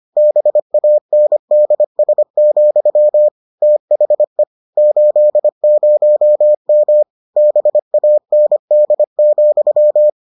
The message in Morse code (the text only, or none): BANDS, THE 80M BAND,